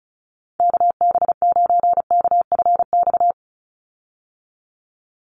Morse code, heard as a keyed tone, about 34 wpm, 700 Hz. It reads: KB9KFX
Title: USA Call Signs 35wpm